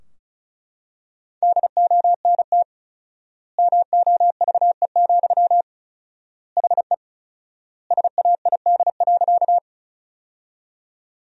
Morse code, heard as a keyed tone, about 36 wpm, 700 Hz. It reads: DONT MOVE, HE SAID.
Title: 4x Word-Spacing - Sentences from Top 300 Words - 35wpm